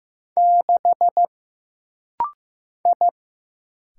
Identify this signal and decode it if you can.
Morse code, heard as a keyed tone, about 15 wpm, 700 Hz. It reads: 6 I
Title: Single Letter-Number Alt - Rapid-Fire 15wpm